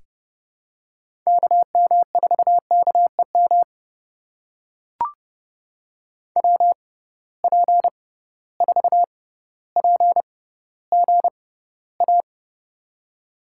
Morse code, heard as a keyed tone, about 30 wpm, spaced as for 15 wpm, 700 Hz. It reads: KM4KEM WP4PGA
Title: USA Call Signs - 6 Characters - 6x Character Spacing - 30wpm